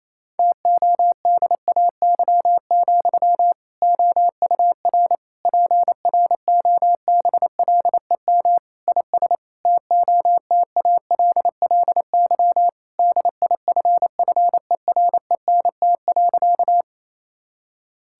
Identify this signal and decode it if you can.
Morse code, heard as a keyed tone, about 28 wpm, 700 Hz. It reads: TODAY, OUR PROBLEM IS TOTALLY DIFFERENT.
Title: Sentences from Top 1600 Words 28wpm